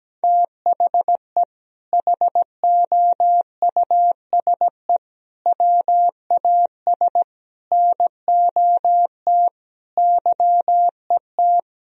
Morse code, heard as a keyed tone, about 17 wpm, 700 Hz. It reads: THE HOUSE WAS NOT YET
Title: Sentences from Top 1000 Words - Review Part 2 - 17wpm